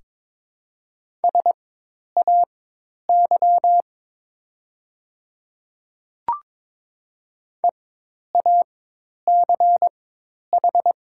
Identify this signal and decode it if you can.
Morse code, heard as a keyed tone, about 22 wpm, spaced as for 10 wpm, 700 Hz. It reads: SAY EACH
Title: 4x Character Spacing - Top 200 Words - 22wpm